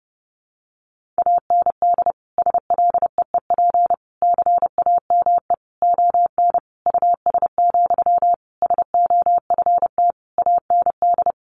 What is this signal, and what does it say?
Morse code, heard as a keyed tone, about 28 wpm, 700 Hz. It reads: AND SLEEP CAME ON US, SOFT AND
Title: Sentences from Top 1500 Words 30wpm